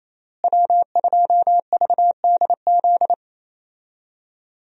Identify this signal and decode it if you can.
Morse code, heard as a keyed tone, about 28 wpm, 700 Hz. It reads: W2VDZ
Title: USA Call Signs - 5 Characters - Repeated 2x - 28wpm